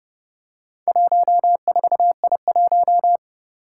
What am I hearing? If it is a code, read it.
Morse code, heard as a keyed tone, about 30 wpm, 700 Hz. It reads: 14I1
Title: Four Challenging Letter-Number Combinations 30wpm